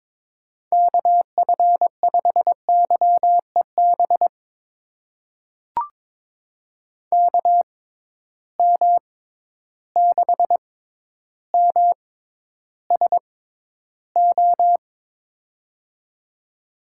Morse code, heard as a keyed tone, about 22 wpm, spaced as for 11 wpm, 700 Hz. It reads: KF5YEB KM6MSO
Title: USA Call Signs - 6 Characters - 6x Character Spacing - 22wpm